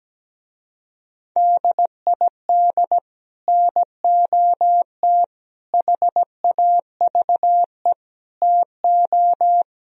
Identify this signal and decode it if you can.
Morse code, heard as a keyed tone, about 17 wpm, 700 Hz. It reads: DID NOT HAVE TO
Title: Speed-Racing - Sets of 4 Words 17wpm